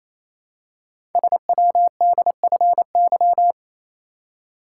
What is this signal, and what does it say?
Morse code, heard as a keyed tone, about 29 wpm, 700 Hz. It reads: SWDFY
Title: Code Groups - 5 Letters - 28wpm